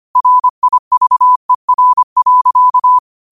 Morse code, heard as a keyed tone, about 25 wpm, 1000 Hz. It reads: RIVER.
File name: qotdfiles / 25wpm